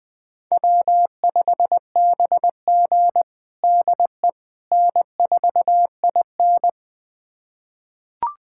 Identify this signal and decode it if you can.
Morse code, heard as a keyed tone, about 21 wpm, 700 Hz. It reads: W5BG DE N4IN
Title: USA Callsign DE Callsign - 4 Characters - 20wpm